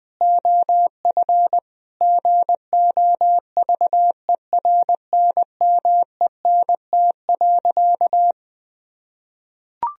Morse code, heard as a keyed tone, about 20 wpm, 700 Hz. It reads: OF GOVERNMENT.
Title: Sentences from Top 1000 Words - Review Part 3 - 20wpm